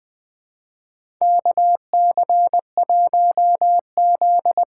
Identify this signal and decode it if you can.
Morse code, heard as a keyed tone, about 21 wpm, 700 Hz. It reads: KC1Z
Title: USA Call Signs - 6x Character Spacing - 20wpm